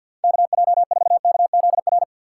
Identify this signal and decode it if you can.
Morse code, heard as a keyed tone, about 50 wpm, 700 Hz. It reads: KJ4KZR
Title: USA Call Signs - 6 Characters - 8x Character Spacing - 50wpm